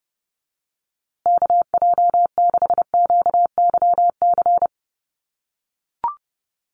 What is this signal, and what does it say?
Morse code, heard as a keyed tone, about 30 wpm, 700 Hz. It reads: KJ6QYC
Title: USA Call Signs Repeated 2x 30wpm